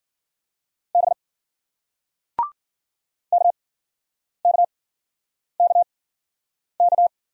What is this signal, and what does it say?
Morse code, heard as a keyed tone, about 53 wpm, spaced as for 9 wpm, 700 Hz. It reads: B KKKK